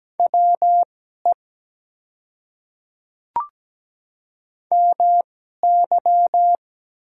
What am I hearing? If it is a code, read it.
Morse code, heard as a keyed tone, about 17 wpm, spaced as for 12 wpm, 700 Hz. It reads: WE MY
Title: 2x Character Spacing - Top 200 Words - 17wpm